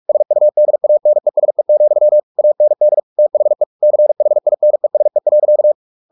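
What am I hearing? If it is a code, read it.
Morse code, heard as a keyed tone, about 44 wpm, 600 Hz. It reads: SUDANESE, AND THE CHINESE.
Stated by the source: Morse practice at 45 wpm, the novel Never by Ken Follett in English, from CW med Gnister.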